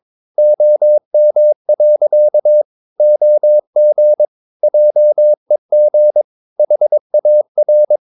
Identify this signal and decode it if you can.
Morse code, heard as a keyed tone, about 22 wpm, 600 Hz. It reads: OM. OG JEG HAR
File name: Ferie QRS 22wpm